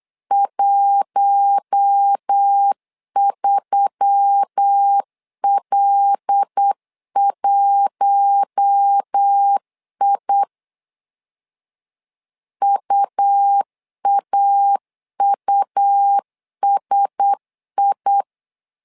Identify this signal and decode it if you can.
Morse code, heard as a keyed tone, about 8 wpm, 800 Hz. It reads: 13L1I UAUSI
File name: LEZIONI / 5 LBVC / 5 L 8wpm 2831